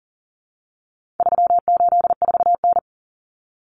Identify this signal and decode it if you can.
Morse code, heard as a keyed tone, about 40 wpm, 700 Hz. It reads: 384N
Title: Common Cut Numbers - 4-Digit 40wpm